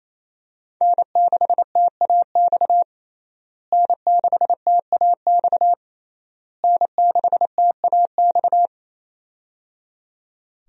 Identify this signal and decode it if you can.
Morse code, heard as a keyed tone, about 27 wpm, 700 Hz. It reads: N6TAX N6TAX N6TAX
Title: USA Call Signs - 5 Characters - Repeated 3x - 28wpm